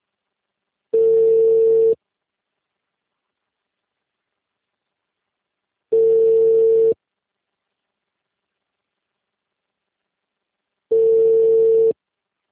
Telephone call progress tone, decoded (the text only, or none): ringback tone